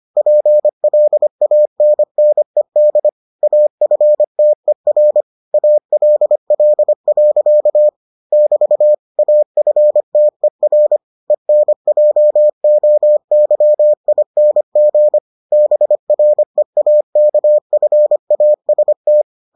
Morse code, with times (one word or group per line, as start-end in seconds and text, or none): 0.2-3.1 PLANNED
3.4-5.2 AFTER
5.5-7.9 ALL.
8.3-8.9 <BT>
9.2-11.0 AFTER
11.3-15.2 ENJOYING
15.5-19.2 BREAKFAST